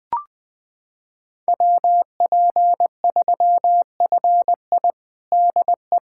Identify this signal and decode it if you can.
Morse code, heard as a keyed tone, about 21 wpm, 700 Hz. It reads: WP3FI DE
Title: USA Callsign DE Callsign - 5 Characters - 20wpm